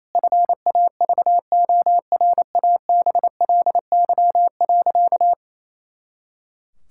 Morse code, heard as a keyed tone, about 28 wpm, 700 Hz. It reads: FAVORABLY.
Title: Pangram Challenge v2 - 28wpm